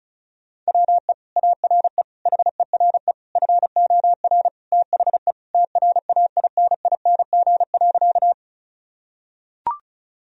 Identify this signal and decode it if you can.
Morse code, heard as a keyed tone, about 36 wpm, 700 Hz. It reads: WE ARE HERE FOR THE TRAINING.
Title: Sentences from Top 1000 Words - Review Part 4 - 35wpm